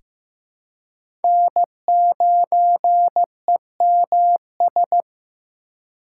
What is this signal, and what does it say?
Morse code, heard as a keyed tone, about 15 wpm, 700 Hz. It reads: N9EMS
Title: USA Call Signs - 4x Character Spacing - 15wpm